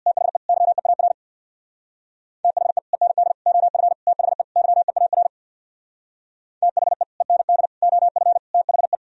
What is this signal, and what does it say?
Morse code, heard as a keyed tone, about 64 wpm, 700 Hz. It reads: THE YEAR THE END OF THE YEAR THE END OF THE